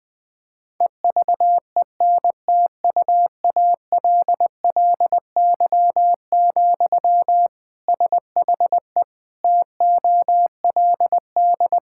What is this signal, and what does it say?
Morse code, heard as a keyed tone, about 20 wpm, 700 Hz. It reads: EVENTUALLY, SHE TOLD
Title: Sentences from Top 1000 Words - Review Part 6 - 20wpm